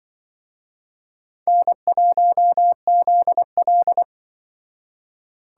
Morse code, heard as a keyed tone, about 24 wpm, 700 Hz. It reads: N1ZL
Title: Speed Racing - USA 1x2 Call Signs 20wpm